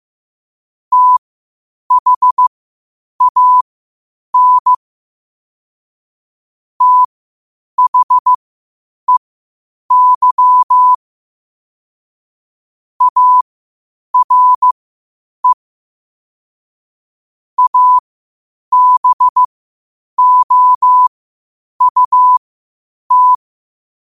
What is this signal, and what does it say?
Morse code, heard as a keyed tone, about 15 wpm, spaced as for 8 wpm, 1000 Hz. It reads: THAN THEY ARE ABOUT
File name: qotdfiles / 8wpm